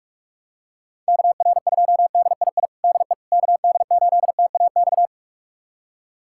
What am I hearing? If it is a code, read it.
Morse code, heard as a keyed tone, about 46 wpm, 700 Hz. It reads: KR2DII DE KD8TAX
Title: USA Callsign DE Callsign 45wpm